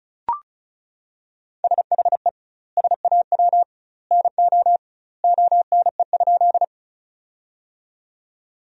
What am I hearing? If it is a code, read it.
Morse code, heard as a keyed tone, about 36 wpm, 700 Hz. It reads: SHE SAW NO ONE?
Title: 2x Word-Spacing - Sentences from Top 300 Words - 35wpm